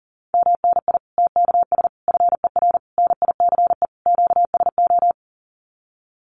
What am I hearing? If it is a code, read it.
Morse code, heard as a keyed tone, about 40 wpm, 700 Hz. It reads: MNI TKS FER NICE QSO
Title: Common QSO Phrases 40wpm